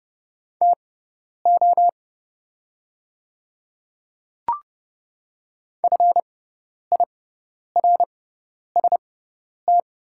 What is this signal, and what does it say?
Morse code, heard as a keyed tone, about 30 wpm, spaced as for 10 wpm, 700 Hz. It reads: TO FIRST
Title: 6x Character Spacing - Top 100 Words - 30wpm